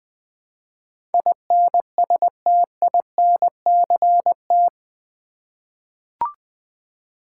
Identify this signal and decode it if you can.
Morse code, heard as a keyed tone, about 21 wpm, 700 Hz. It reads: INSTINCT
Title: Eight Letter Words - 20wpm